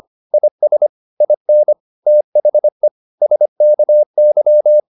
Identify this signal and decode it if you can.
Morse code, heard as a keyed tone, about 25 wpm, 600 Hz. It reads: IS IN THE SKY